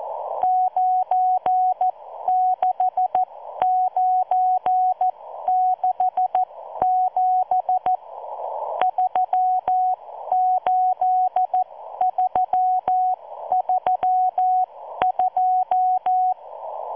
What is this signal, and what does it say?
Morse code, heard as a keyed tone, about 14 wpm, spaced as for 12 wpm, 740 Hz. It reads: 96967 38332